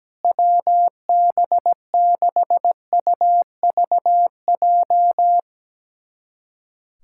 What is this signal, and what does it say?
Morse code, heard as a keyed tone, about 17 wpm, 700 Hz. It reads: WB6UVJ
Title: USA Callsign DE Callsign - 6 Characters - 17wpm